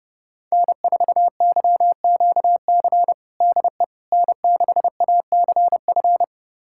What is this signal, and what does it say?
Morse code, heard as a keyed tone, about 30 wpm, 700 Hz. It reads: N4YQC DE N6ACF
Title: USA Callsign DE Callsign - 5 Characters - 30wpm